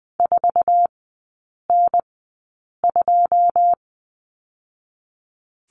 Morse code, heard as a keyed tone, about 20 wpm, spaced as for 8 wpm, 700 Hz. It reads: 4N2
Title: 081 Lesson 14e - Number 2 - Three Characters 20wpm